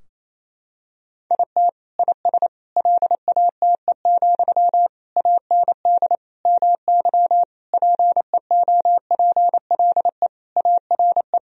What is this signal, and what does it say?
Morse code, heard as a keyed tone, about 28 wpm, 700 Hz. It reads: IT IS LATE, AND MY PEOPLE ARE
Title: Sentences from Top 1000 Words - Review Part 3 - 28wpm